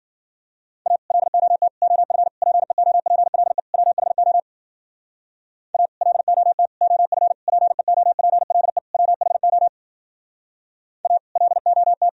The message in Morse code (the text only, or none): A LOT OF PEOPLE WHO A LOT OF PEOPLE WHO A LOT